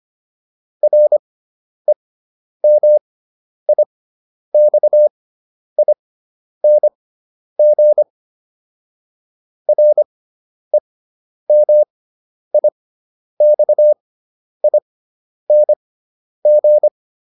Morse code, heard as a keyed tone, about 25 wpm, spaced as for 10 wpm, 600 Hz. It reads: REMIXING REMIXING